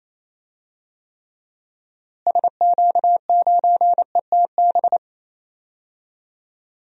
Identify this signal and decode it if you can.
Morse code, heard as a keyed tone, about 28 wpm, 700 Hz. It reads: SQ9ETB
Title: DXCC-100 Call Signs - 6 Characters - Repeated 2x - 28wpm